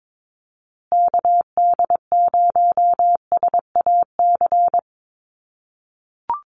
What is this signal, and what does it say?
Morse code, heard as a keyed tone, about 22 wpm, 700 Hz. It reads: KD0SAC
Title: USA Call Signs Repeated 2x 22wpm